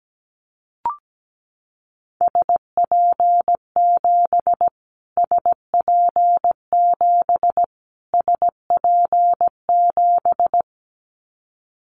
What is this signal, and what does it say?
Morse code, heard as a keyed tone, about 17 wpm, 700 Hz. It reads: SP7 SP7 SP7
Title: Callsign Prefixes for Top 25 DXCC - Repeated 3x - 17wpm